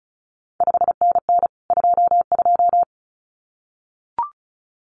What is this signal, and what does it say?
Morse code, heard as a keyed tone, about 36 wpm, 700 Hz. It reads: 5NN 22
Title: CQ WW DX Contest Exchange Encore - 35wpm